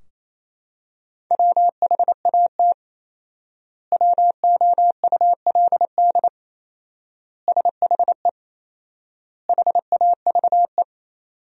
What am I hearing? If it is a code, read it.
Morse code, heard as a keyed tone, about 28 wpm, 700 Hz. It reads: WHAT WOULD SHE HAVE
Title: 4x Word-Spacing - Sentences from Top 100 Words - 28wpm